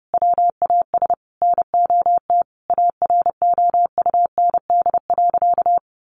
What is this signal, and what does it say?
Morse code, heard as a keyed tone, about 28 wpm, 700 Hz. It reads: WAS NOT AROUND.
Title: Sentences from Top 1300 Words 30wpm